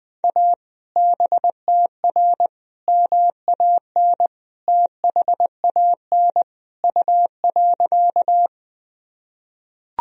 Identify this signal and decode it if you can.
Morse code, heard as a keyed tone, about 21 wpm, 700 Hz. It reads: A BTR MAN THAN U.
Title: Sentences from Top 400 Words and Abbreviations 20wpm